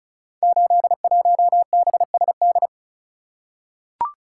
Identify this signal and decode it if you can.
Morse code, heard as a keyed tone, about 36 wpm, 700 Hz. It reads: 81BSD
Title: Code Groups - 5 Letter:Numbers - 35wpm